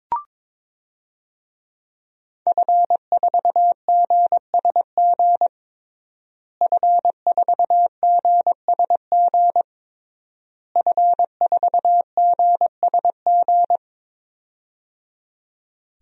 Morse code, heard as a keyed tone, about 22 wpm, 700 Hz. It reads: F4GSG F4GSG F4GSG
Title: DXCC-100 Call Signs - 5 Characters - Repeated 3x - 22wpm